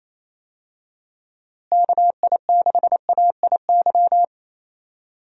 Morse code, heard as a keyed tone, about 27 wpm, 700 Hz. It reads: KI6AIY
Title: USA Call Signs - 6 Characters - Repeated 3x - 28wpm